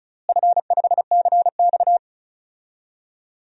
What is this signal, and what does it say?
Morse code, heard as a keyed tone, about 36 wpm, 700 Hz. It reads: F5CX
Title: DXCC-100 Call Signs - 4 Characters - Repeated 2x - 35wpm